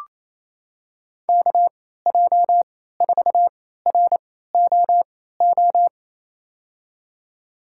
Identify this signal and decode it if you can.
Morse code, heard as a keyed tone, about 27 wpm, spaced as for 16 wpm, 700 Hz. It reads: KJ4ROO
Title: USA Call Signs - 6 Characters - 3x Character Spacing - 28wpm